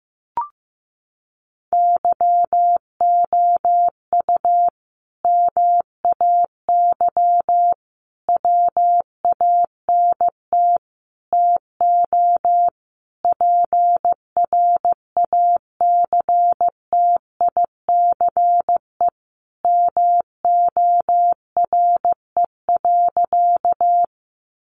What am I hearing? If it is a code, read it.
Morse code, heard as a keyed tone, about 15 wpm, 700 Hz. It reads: YOU MAY WANT TO PRACTICE MORE.
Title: Sentences from Top 700 Words 15wpm